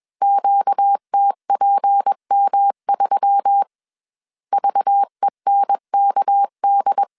Morse code, heard as a keyed tone, about 21 wpm, 800 Hz. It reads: QTPM3 VENKD
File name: CW / DET / 1 6 1500C S B 20wpm 2048